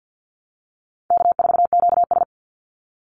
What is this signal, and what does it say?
Morse code, heard as a keyed tone, about 50 wpm, 700 Hz. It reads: K4QS
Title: NA SOTA Chaser Callsigns 50wpm